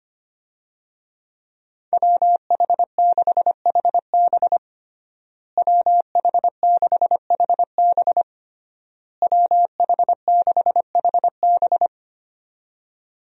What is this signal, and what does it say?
Morse code, heard as a keyed tone, about 25 wpm, 700 Hz. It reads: WH6HB WH6HB WH6HB
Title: USA Call Signs - 5 Characters - Repeated 3x - 25wpm